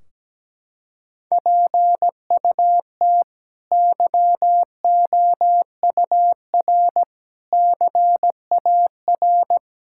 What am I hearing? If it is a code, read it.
Morse code, heard as a keyed tone, about 17 wpm, 700 Hz. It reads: PUT YOUR CAR
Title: Sentences from Top 1600 Words 17wpm